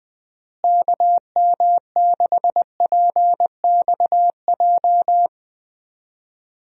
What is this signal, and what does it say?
Morse code, heard as a keyed tone, about 20 wpm, 700 Hz. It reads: KM6PXJ
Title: USA Call Signs - 6 Characters - Repeated 3x - 20wpm